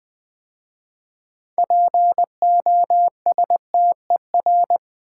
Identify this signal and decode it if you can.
Morse code, heard as a keyed tone, about 20 wpm, 700 Hz. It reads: POSTER
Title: 6 Letter Words - First 3 Letters Given - 20wpm